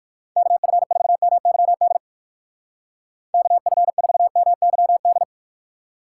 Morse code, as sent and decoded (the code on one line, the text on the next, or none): -.- ..-. ....- -- -.-- -.. / -.- ..-. ....- -- -.-- -..
KF4MYD KF4MYD